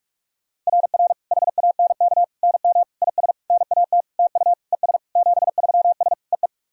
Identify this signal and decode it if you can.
Morse code, heard as a keyed tone, about 41 wpm, 690 Hz. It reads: RR HANK NM IS NAT TU ES 73S EE
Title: SOTA QSO Practice 45wpm